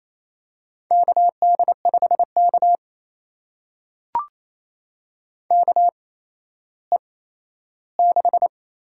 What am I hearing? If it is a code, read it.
Morse code, heard as a keyed tone, about 28 wpm, spaced as for 14 wpm, 700 Hz. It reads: KD5K KE6
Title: USA Call Signs - 8x Character Spacing - 28wpm